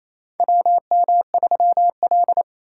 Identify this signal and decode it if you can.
Morse code, heard as a keyed tone, about 28 wpm, 700 Hz. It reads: WM3L
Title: USA Call Signs - 4 Characters - 3x Character Spacing - 28wpm